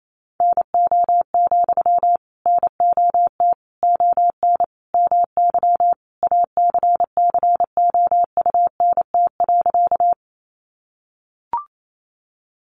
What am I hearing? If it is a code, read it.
Morse code, heard as a keyed tone, about 28 wpm, 700 Hz. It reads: NO, NOT ON MY ACCOUNT.
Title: Sentences from Top 900 Words 28wpm